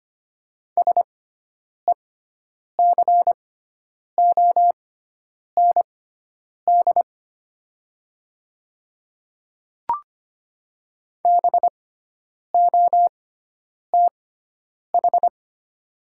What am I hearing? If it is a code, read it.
Morse code, heard as a keyed tone, about 25 wpm, spaced as for 9 wpm, 700 Hz. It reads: SECOND BOTH
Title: 6x Character Spacing - Top 300 Words - 25wpm